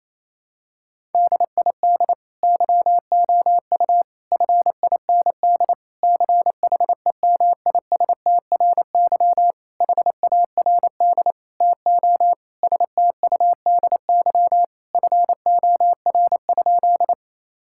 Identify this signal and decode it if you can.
Morse code, heard as a keyed tone, about 28 wpm, 700 Hz. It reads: DID YOU FIND CHEMISTRY HARD TO STUDY FOR?
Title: Sentences from Top 1000 Words - Review Part 6 - 28wpm